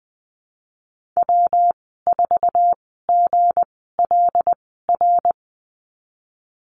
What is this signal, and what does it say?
Morse code, heard as a keyed tone, about 20 wpm, spaced as for 15 wpm, 700 Hz. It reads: W4GLR
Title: 200 Lesson 32g - Letter k - Call Signs 20wpm